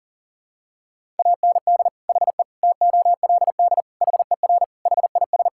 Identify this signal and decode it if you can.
Morse code, heard as a keyed tone, about 40 wpm, 700 Hz. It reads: AND HE TOLD HER HIS
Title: Sentences from Top 1000 Words - Review Part 5 - 40wpm